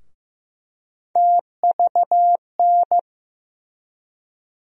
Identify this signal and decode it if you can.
Morse code, heard as a keyed tone, about 15 wpm, 700 Hz. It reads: TVN
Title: USA Call Sign Postfixes - 3 Letters 15wpm